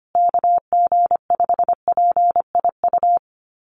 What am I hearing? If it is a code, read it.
Morse code, heard as a keyed tone, about 25 wpm, 700 Hz. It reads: KG5PIU
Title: USA Call Signs Repeated 3x 25wpm